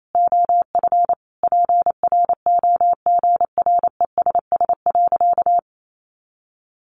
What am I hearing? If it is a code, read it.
Morse code, heard as a keyed tone, about 28 wpm, 700 Hz. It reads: OF PROGRESS.
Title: Sentences from Top 1400 Words 28wpm